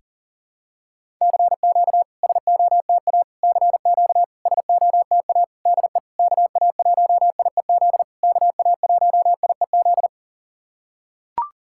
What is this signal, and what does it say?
Morse code, heard as a keyed tone, about 40 wpm, 690 Hz. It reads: CQ SOTA CQ SOTA DE KA1IEZ KA1IEZ
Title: SOTA QSO Practice 40wpm